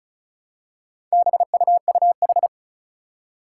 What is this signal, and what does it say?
Morse code, heard as a keyed tone, about 34 wpm, 700 Hz. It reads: BUUH
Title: Four Challenging Letter-Number Combinations 35wpm